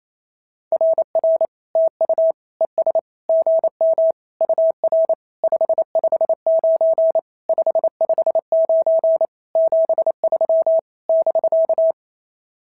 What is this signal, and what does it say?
Morse code, heard as a keyed tone, about 27 wpm, 650 Hz. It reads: RR TU ES GM UR 559 559 73 <BK>
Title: SOTA QSO Practice 28wpm